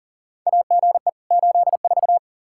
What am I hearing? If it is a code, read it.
Morse code, heard as a keyed tone, about 40 wpm, 700 Hz. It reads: AGE 84
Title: Practice QSO - Exchange 3 - More Info - 40wpm